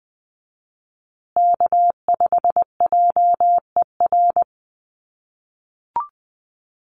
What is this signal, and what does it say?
Morse code, heard as a keyed tone, about 20 wpm, 700 Hz. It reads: K5JER
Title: USA Call Signs Repeated 3x 20wpm